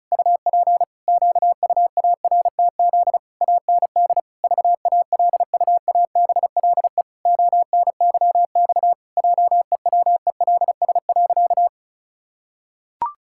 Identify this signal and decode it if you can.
Morse code, heard as a keyed tone, about 36 wpm, 700 Hz. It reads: UP QUARTZ AND VALUABLE ONYX JEWELS.
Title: Pangram Challenge v2 - 35wpm